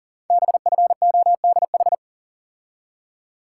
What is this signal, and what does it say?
Morse code, heard as a keyed tone, about 38 wpm, 700 Hz. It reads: BFODH
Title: Code Groups - 5 Letters - 40wpm